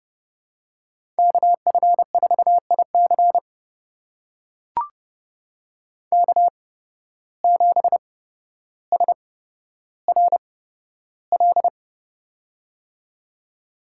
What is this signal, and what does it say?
Morse code, heard as a keyed tone, about 30 wpm, spaced as for 13 wpm, 700 Hz. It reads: KF4IC K7SRL
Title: USA Call Signs - 5 Characters - 8x Character Spacing - 30wpm